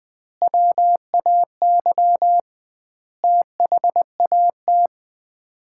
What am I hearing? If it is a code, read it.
Morse code, heard as a keyed tone, about 20 wpm, 700 Hz. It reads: WAY THAT
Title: 2x Word-Spacing - Sets of 5 Words - 20wpm